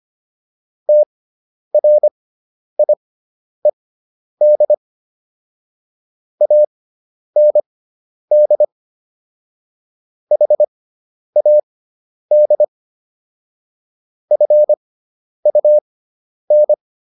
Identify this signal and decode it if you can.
Morse code, heard as a keyed tone, about 25 wpm, spaced as for 10 wpm, 600 Hz. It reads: TRIED AND HAD FUN